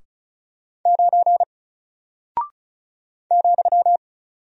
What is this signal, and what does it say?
Morse code, heard as a keyed tone, about 36 wpm, 700 Hz. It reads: 9 ,
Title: Single Character (Letter:Number:Punctuation:Prosign) Alt 35wpm